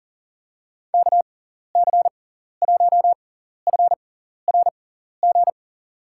Morse code, heard as a keyed tone, about 40 wpm, spaced as for 14 wpm, 700 Hz. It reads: KC1FRG
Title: USA Call Signs - 6 Characters - 6x Character Spacing - 40wpm